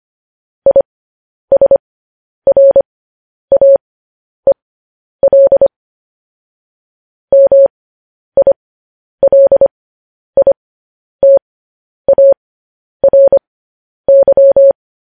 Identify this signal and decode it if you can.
Morse code, heard as a keyed tone, about 25 wpm, spaced as for 10 wpm, 550 Hz. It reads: ISRAEL MILITARY
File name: podcast / News headlines-1x-25x10